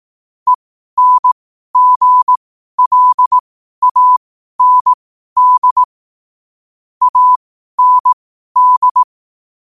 Morse code, heard as a keyed tone, about 18 wpm, spaced as for 12 wpm, 1000 Hz. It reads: ENGLAND AND
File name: qotdfiles / 12wpm